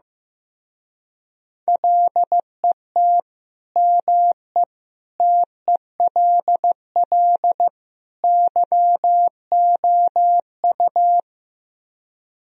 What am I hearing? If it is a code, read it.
Morse code, heard as a keyed tone, about 15 wpm, 700 Hz. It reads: LET ME TELL YOU
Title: Speed-Racing - Sets of 4 Words 15wpm